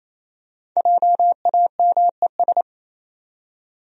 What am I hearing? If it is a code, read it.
Morse code, heard as a keyed tone, about 28 wpm, 700 Hz. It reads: JAMES
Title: USA Common Names - 28wpm